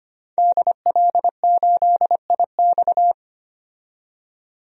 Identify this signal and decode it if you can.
Morse code, heard as a keyed tone, about 25 wpm, 700 Hz. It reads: DL8IX
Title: DXCC-100 Call Signs - 5 Characters - 25wpm